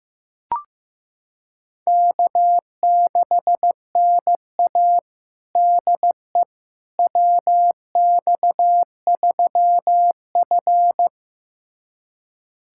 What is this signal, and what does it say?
Morse code, heard as a keyed tone, about 15 wpm, 700 Hz. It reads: K6NA DE WX3F
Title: USA Callsign DE Callsign - 4 Characters - 15wpm